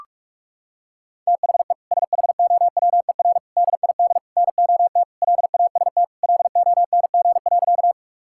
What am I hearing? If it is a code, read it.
Morse code, heard as a keyed tone, about 46 wpm, 700 Hz. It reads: THE SHOWER DID NOT LAST LONG.
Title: Sentences from Top 1600 Words 45wpm